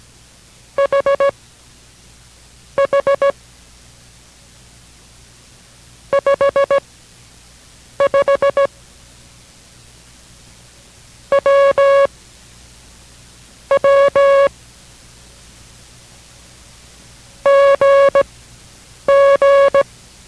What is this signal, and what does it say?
Morse code, heard as a keyed tone, about 17 wpm, spaced as for 6 wpm, 560 Hz. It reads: HH 55 W W GG